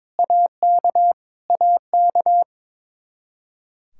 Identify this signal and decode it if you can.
Morse code, heard as a keyed tone, about 21 wpm, 700 Hz. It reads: AK AK
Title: Practice QSO - Exchange 1 - Beginning - 22wpm